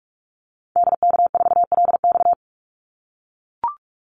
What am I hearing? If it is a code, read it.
Morse code, heard as a keyed tone, about 44 wpm, 700 Hz. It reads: DK4LX
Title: Call Signs - 45wpm